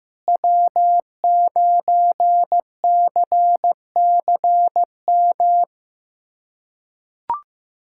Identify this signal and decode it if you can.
Morse code, heard as a keyed tone, about 15 wpm, 700 Hz. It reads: W9CCM
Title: USA Callsign DE Callsign - 5 Characters - 15wpm